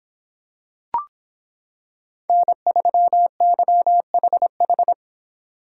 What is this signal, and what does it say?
Morse code, heard as a keyed tone, about 25 wpm, 700 Hz. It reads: N3YHH